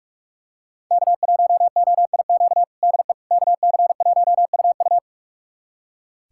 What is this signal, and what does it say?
Morse code, heard as a keyed tone, about 41 wpm, 700 Hz. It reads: K1OIQ DE KC1UU
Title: USA Callsign DE Callsign - 5 Characters - 45wpm